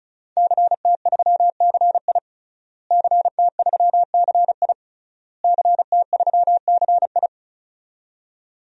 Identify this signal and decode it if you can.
Morse code, heard as a keyed tone, about 35 wpm, 700 Hz. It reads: CT3CI CT3CI CT3CI
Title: DXCC-100 Call Signs - 5 Characters - Repeated 3x - 35wpm